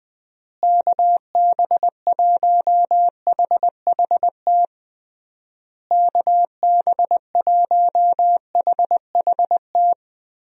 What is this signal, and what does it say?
Morse code, heard as a keyed tone, about 20 wpm, 700 Hz. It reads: KB1HHT KB1HHT
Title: USA Call Signs - 6 Characters - Repeated 3x - 20wpm